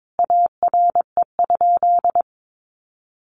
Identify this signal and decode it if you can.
Morse code, heard as a keyed tone, about 22 wpm, 700 Hz. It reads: ARE?
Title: A Sound Of Thunder 22wpm